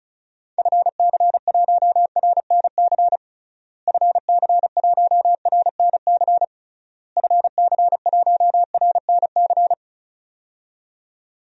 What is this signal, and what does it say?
Morse code, heard as a keyed tone, about 34 wpm, 700 Hz. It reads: FC1RNC FC1RNC FC1RNC
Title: DXCC-100 Call Signs - 6 Characters - Repeated 3x - 35wpm